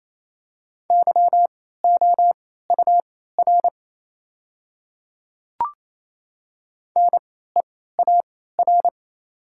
Morse code, heard as a keyed tone, about 27 wpm, spaced as for 16 wpm, 700 Hz. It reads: YOUR NEAR
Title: 3x Character Spacing - Top 200 Words - 28wpm